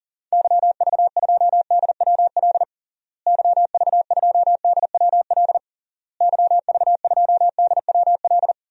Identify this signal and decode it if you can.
Morse code, heard as a keyed tone, about 37 wpm, 700 Hz. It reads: YV2DWL YV2DWL YV2DWL
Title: DXCC-100 Call Signs - 6 Characters - Repeated 3x - 40wpm